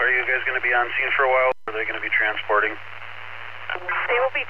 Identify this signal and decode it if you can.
Emergency Alert System (SAME) tones, heard as harsh buzzing bursts, with speech. It is heard from 0.2 to 4.0 s.